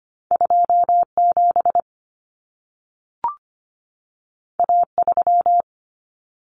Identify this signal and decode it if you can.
Morse code, heard as a keyed tone, about 25 wpm, 700 Hz. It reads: 27 A3
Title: Common Cut Numbers - 2-Digit 25wpm